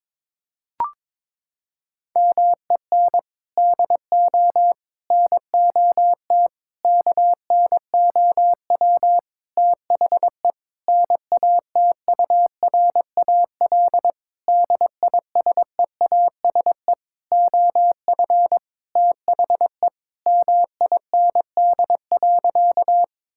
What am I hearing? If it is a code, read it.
Morse code, heard as a keyed tone, about 21 wpm, 700 Hz. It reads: MEN DO NOT KNOW THE NATURAL DISEASE OF THE MIND.
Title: Sentences from Top 1000 Words - Review Part 6 - 22wpm